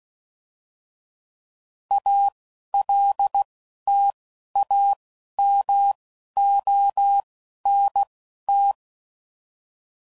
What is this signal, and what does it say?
Morse code, heard as a keyed tone, about 16 wpm, spaced as for 11 wpm, 800 Hz. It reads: ALTAMONT
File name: CW Code Groups @10WPM / fromSM's / Cities